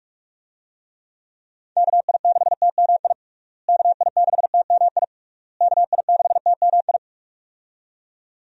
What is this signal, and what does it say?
Morse code, heard as a keyed tone, about 41 wpm, 700 Hz. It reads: KI6TMI KI6TMI KI6TMI
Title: USA Call Signs - 6 Characters - Repeated 3x - 45wpm